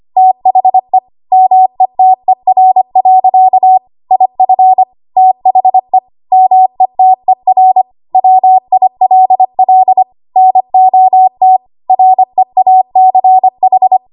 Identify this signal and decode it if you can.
Morse code, heard as a keyed tone, about 25 wpm, 750 Hz. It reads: THE METER. IF THE METER WILL NOT REACH
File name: Archive / 25 WPM / 25WPM